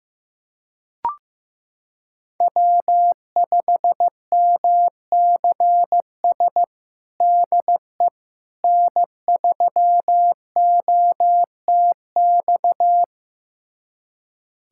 Morse code, heard as a keyed tone, about 16 wpm, 700 Hz. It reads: W5MCS DE N3OTX
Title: USA Callsign DE Callsign - 5 Characters - 15wpm